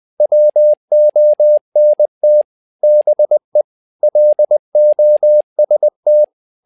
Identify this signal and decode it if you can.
Morse code, heard as a keyed tone, about 20 wpm, 600 Hz. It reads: WONT BE LOST
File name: learning center / tower trouble / tower trouble 20wpm